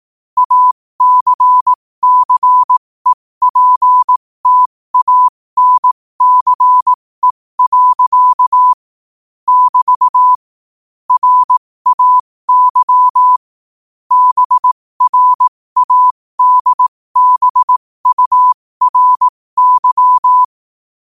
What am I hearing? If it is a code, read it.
Morse code, heard as a keyed tone, about 18 wpm, spaced as for 15 wpm, 1000 Hz. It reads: ACCEPTANCE. <BT> RAY BRADBURY
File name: qotdfiles / 15wpm